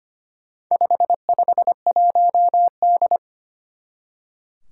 Morse code, heard as a keyed tone, about 25 wpm, 700 Hz. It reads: 551D
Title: Four Challenging Letter-Number Combinations 25wpm